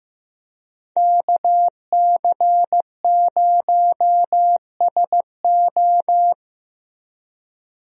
Morse code, heard as a keyed tone, about 15 wpm, 700 Hz. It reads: KC0SO
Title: USA Call Signs - 5 Characters - Repeated 2x - 15wpm